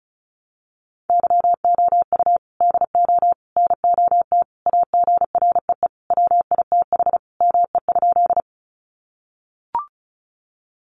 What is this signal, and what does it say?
Morse code, heard as a keyed tone, about 34 wpm, 700 Hz. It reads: YOU DO NOT AGREE WITH ME?
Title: Sentences from Top 1100 Words 35wpm